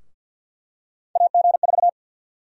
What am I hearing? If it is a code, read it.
Morse code, heard as a keyed tone, about 51 wpm, 700 Hz. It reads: AG4
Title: USA Call Sign Prefixes - Repeated 2x - 50wpm